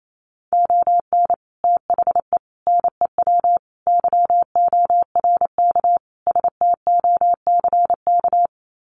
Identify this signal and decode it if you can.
Morse code, heard as a keyed tone, about 28 wpm, 690 Hz. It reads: ON THE NEW YORK STOCK
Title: Sets of 5 Words Encore 28wpm